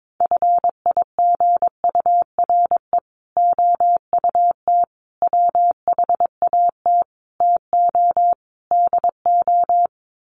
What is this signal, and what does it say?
Morse code, heard as a keyed tone, about 22 wpm, 700 Hz. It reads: FIGURE OUT WHAT TO DO
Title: Sets of 5 Words Encore2 22wpm